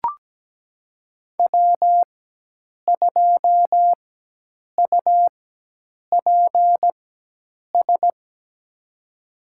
Morse code, heard as a keyed tone, about 17 wpm, spaced as for 8 wpm, 700 Hz. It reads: W2UPS
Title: USA Call Signs - 4x Character Spacing - 17wpm